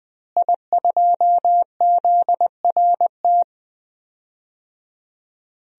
Morse code, heard as a keyed tone, about 20 wpm, 700 Hz. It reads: I2ZRT
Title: DXCC-100 Call Signs - 5 Characters - Repeated 3x - 20wpm